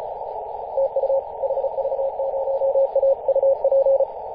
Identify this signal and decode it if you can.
Morse code, heard as a keyed tone, about 34 wpm, 540 Hz, with radio noise. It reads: TU HV0A UP
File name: 2016/Apr/09 0838z HV0A 18.071MHz CW